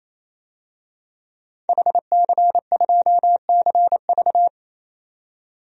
Morse code, heard as a keyed tone, about 28 wpm, 700 Hz. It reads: HC2CV
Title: DXCC-100 Call Signs - 5 Characters - 28wpm